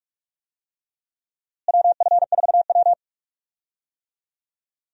Morse code, heard as a keyed tone, about 46 wpm, 700 Hz. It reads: WF4W
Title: USA Call Signs - 4 Characters - Repeated 2x - 45wpm